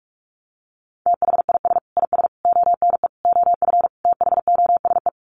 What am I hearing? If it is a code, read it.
Morse code, heard as a keyed tone, about 44 wpm, 700 Hz. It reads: THIS IS ONE OF THOSE
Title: Sets of 5 Words Encore2 45wpm